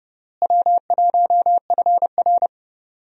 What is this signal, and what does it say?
Morse code, heard as a keyed tone, about 30 wpm, 700 Hz. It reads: W1FR
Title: USA Call Signs - 4 Characters - Repeated 3x - 30wpm